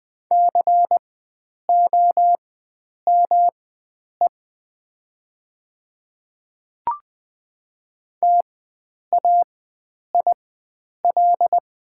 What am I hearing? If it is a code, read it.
Morse code, heard as a keyed tone, about 20 wpm, spaced as for 9 wpm, 700 Hz. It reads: COME TAIL
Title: 4x Character Spacing - Top 500 Words - 20wpm